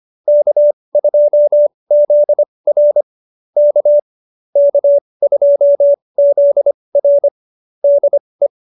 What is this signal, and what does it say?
Morse code, heard as a keyed tone, about 25 wpm, spaced as for 20 wpm, 590 Hz. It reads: K2ZR K K2ZR DE
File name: QSO302 20